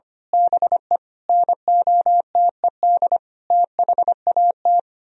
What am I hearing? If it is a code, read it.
Morse code, heard as a keyed tone, about 24 wpm, 700 Hz. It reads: BE NOTED THAT
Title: Speed-Racing - Sets of 5 Words 25wpm